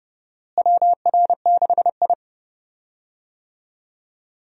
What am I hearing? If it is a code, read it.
Morse code, heard as a keyed tone, about 30 wpm, 700 Hz. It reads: WR6I
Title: USA Call Signs - 4 Characters - Repeated 2x - 30wpm